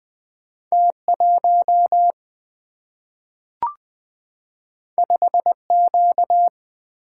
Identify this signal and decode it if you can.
Morse code, heard as a keyed tone, about 20 wpm, 700 Hz. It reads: T1 5Q
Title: Single-Letter Single-Number 20wpm